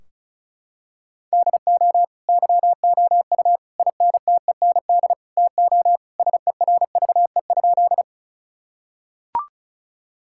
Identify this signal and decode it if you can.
Morse code, heard as a keyed tone, about 36 wpm, 700 Hz. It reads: DO YOU INTEND TO SERVE?
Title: Sentences from Top 1600 Words 35wpm